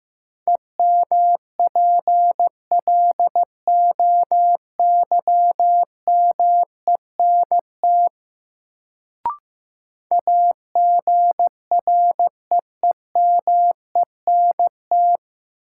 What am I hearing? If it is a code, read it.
Morse code, heard as a keyed tone, about 15 wpm, 700 Hz. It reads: EMPLOYMENT AGREEMENT
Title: Sentences from Top 1500 Words - New Word Drill - 15wpm